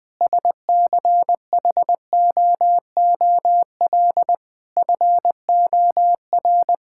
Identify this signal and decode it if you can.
Morse code, heard as a keyed tone, about 20 wpm, 700 Hz. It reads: SCHOOL FOR
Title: Sentences from Top 1000 Words - Review Part 6 - 20wpm